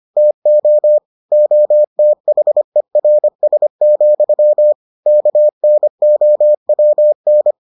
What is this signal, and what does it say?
Morse code, heard as a keyed tone, about 25 wpm, 600 Hz. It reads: TO OTHERS, KNOWN